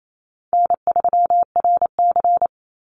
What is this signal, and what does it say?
Morse code, heard as a keyed tone, about 28 wpm, 700 Hz. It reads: N3RC
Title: Easy Call Signs Repeated 2x - 28wpm